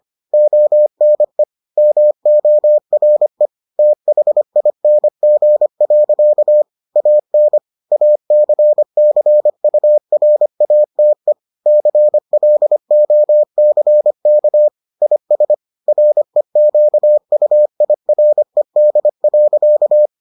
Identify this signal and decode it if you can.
Morse code, heard as a keyed tone, about 25 wpm, 600 Hz. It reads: ONE MORE THING. AN ACCURATE CLOCK IS REQUIRED.